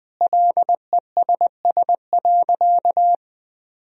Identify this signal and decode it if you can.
Morse code, heard as a keyed tone, about 20 wpm, 700 Hz. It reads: LESS.
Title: Sentences from Top 1000 Words - Review Part 3 - 20wpm